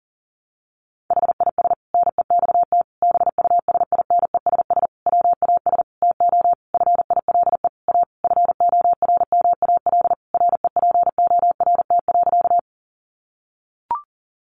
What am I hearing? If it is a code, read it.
Morse code, heard as a keyed tone, about 40 wpm, 700 Hz. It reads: HIS NEXT BUSINESS WAS TO FILE A FORMAL REPORT.
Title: Sentences from Top 1300 Words 40wpm